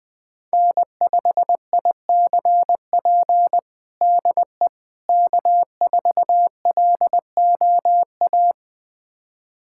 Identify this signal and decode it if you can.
Morse code, heard as a keyed tone, about 20 wpm, 700 Hz. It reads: N5ICP DE K4LOA
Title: USA Callsign DE Callsign - 5 Characters - 20wpm